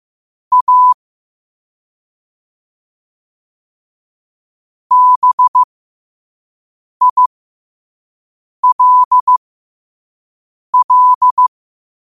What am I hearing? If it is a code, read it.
Morse code, heard as a keyed tone, about 15 wpm, spaced as for 5 wpm, 1000 Hz. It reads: A BILL
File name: qotdfiles / 5wpm